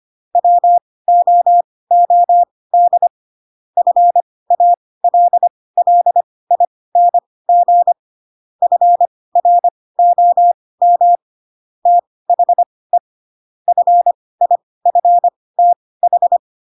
Morse code, heard as a keyed tone, about 25 wpm, spaced as for 18 wpm, 700 Hz. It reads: WOOD FALLING FROM THE FIFTH
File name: Pinocchio / Pinocchio ch7 12 25 ik1hge